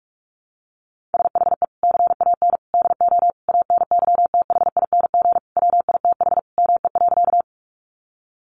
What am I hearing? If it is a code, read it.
Morse code, heard as a keyed tone, about 44 wpm, 700 Hz. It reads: SHE CAN DO ANYTHING WITH ME.
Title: Sentences from Top 1000 Words 45wpm